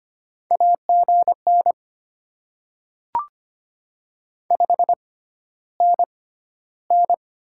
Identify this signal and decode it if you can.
Morse code, heard as a keyed tone, about 25 wpm, spaced as for 13 wpm, 700 Hz. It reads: AGN 5NN
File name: Top 20 QSO Elements - 6x Character Spacing - 25wpm